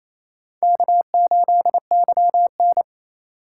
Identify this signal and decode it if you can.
Morse code, heard as a keyed tone, about 36 wpm, 700 Hz. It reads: K8YN